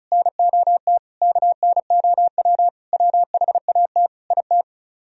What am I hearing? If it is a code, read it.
Morse code, heard as a keyed tone, about 32 wpm, 700 Hz. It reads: NOT KNOW WHAT IT
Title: Speed-Racing - Sets of 5 Words 35wpm